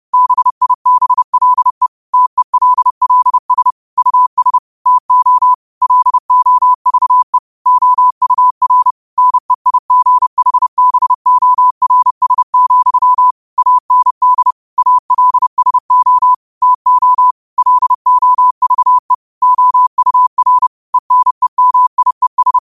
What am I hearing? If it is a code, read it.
Morse code, heard as a keyed tone, about 30 wpm, 1000 Hz. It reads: BIBLE TELLS US TO LOVE OUR NEIGHBORS, AND ALSO TO LOVE OUR ENEMIES